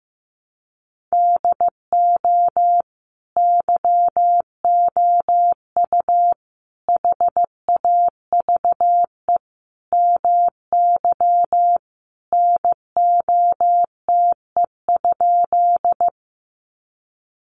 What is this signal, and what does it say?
Morse code, heard as a keyed tone, about 15 wpm, 700 Hz. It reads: DO YOU HAVE MY NOTE?
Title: Sentences from Top 500 Words 15wpm